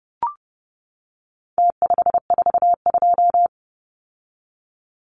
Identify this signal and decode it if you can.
Morse code, heard as a keyed tone, about 30 wpm, 700 Hz. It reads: T542
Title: Numbers - Contest Style - 4-Digit 30wpm